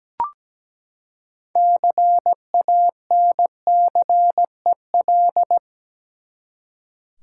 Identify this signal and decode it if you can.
Morse code, heard as a keyed tone, about 17 wpm, 700 Hz. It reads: CANCEL
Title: Six Letter Words - 17wpm